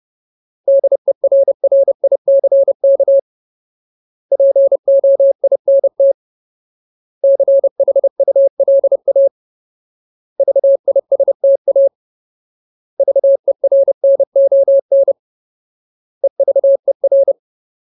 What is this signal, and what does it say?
Morse code, heard as a keyed tone, about 30 wpm, 550 Hz. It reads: DERRICK POINT CHULA VISTA VERNON EVER
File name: Ultimate Word List / 30WPM Ultimate